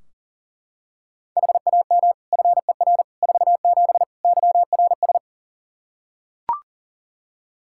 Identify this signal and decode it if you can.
Morse code, heard as a keyed tone, about 40 wpm, 700 Hz. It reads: HAM FER 47 YRS
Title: K1USN SST QSO Practice 40wpm